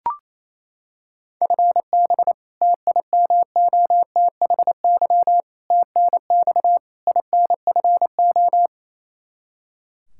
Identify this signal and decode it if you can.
Morse code, heard as a keyed tone, about 28 wpm, 700 Hz. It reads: FB TIMOTHY TNX INFO
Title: Practice QSO - Exchange 3 - More Info - 28wpm